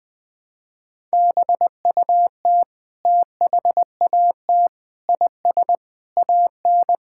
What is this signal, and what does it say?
Morse code, heard as a keyed tone, about 21 wpm, 700 Hz. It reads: BUT THAT IS AN
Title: Sentences from Top 1000 Words - Review Part 6 - 20wpm